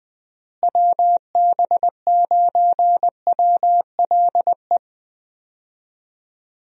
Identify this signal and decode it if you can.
Morse code, heard as a keyed tone, about 20 wpm, 700 Hz. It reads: WB9WLE
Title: USA Call Signs - 6 Characters - Repeated 3x - 20wpm